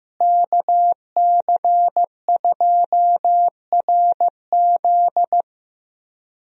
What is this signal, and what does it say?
Morse code, heard as a keyed tone, about 15 wpm, 700 Hz. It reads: KC2RZ
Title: USA Call Signs - 5 Characters - Repeated 2x - 15wpm